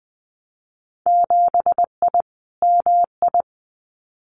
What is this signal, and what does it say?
Morse code, heard as a keyed tone, about 19 wpm, 700 Hz. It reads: 7I MI
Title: Winter Field Day Exchanges Encore 20wpm